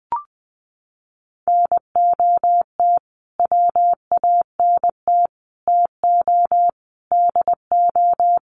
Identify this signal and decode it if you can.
Morse code, heard as a keyed tone, about 20 wpm, 700 Hz. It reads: NOT WANT TO DO
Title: Sets of 5 Words Encore 20wpm